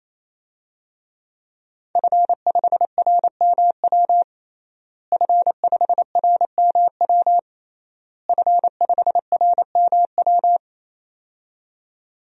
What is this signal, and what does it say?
Morse code, heard as a keyed tone, about 27 wpm, 700 Hz. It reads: F5RMW F5RMW F5RMW
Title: DXCC-100 Call Signs - 5 Characters - Repeated 3x - 28wpm